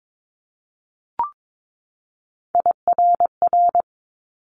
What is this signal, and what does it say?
Morse code, heard as a keyed tone, about 22 wpm, 700 Hz. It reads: IRR
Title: Multi-letter Phonemes 22wpm